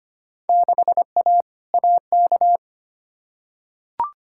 Morse code, heard as a keyed tone, about 25 wpm, 700 Hz. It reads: 6A AK
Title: Field Day Exchanges Encore - 25wpm